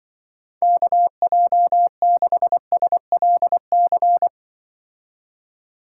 Morse code, heard as a keyed tone, about 24 wpm, 700 Hz. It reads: KJ6SLC
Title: Speed Racing - USA 2x3 Call Signs 20wpm